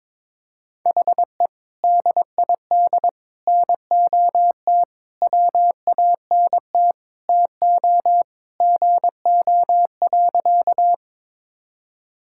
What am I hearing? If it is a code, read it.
Morse code, heard as a keyed tone, about 22 wpm, 700 Hz. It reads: HE DID NOT WANT TO GO.
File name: practice / 22wpm / Sentences with 3-Word Phrases - 22wpm